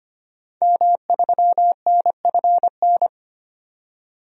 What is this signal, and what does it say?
Morse code, heard as a keyed tone, about 25 wpm, 700 Hz. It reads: M3NFN
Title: DXCC-100 Call Signs - 5 Characters - 25wpm